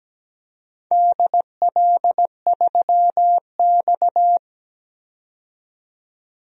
Morse code, heard as a keyed tone, about 17 wpm, 700 Hz. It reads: DL3X
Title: DXCC-100 Call Signs - 4 Characters - 17wpm